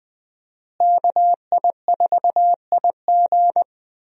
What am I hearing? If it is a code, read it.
Morse code, heard as a keyed tone, about 21 wpm, 700 Hz. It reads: KI4IG
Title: USA Call Signs - 5 Characters - Repeated 3x - 20wpm